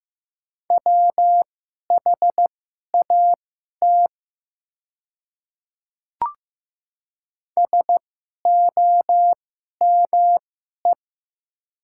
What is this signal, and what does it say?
Morse code, heard as a keyed tone, about 15 wpm, spaced as for 11 wpm, 700 Hz. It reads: WHAT SOME
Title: 2x Character Spacing - Top 100 Words - 15wpm